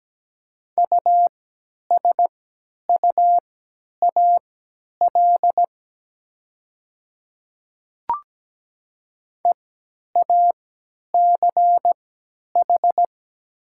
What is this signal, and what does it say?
Morse code, heard as a keyed tone, about 17 wpm, spaced as for 10 wpm, 690 Hz. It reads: USUAL EACH
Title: 3x Character Spacing - Top 500 Words - 17wpm